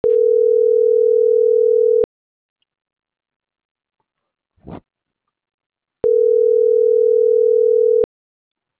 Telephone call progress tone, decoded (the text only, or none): ringback tone